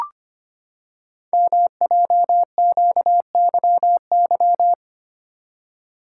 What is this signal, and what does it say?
Morse code, heard as a keyed tone, about 26 wpm, 700 Hz. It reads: MJQYY
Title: Code Groups - 5 Letters - 25wpm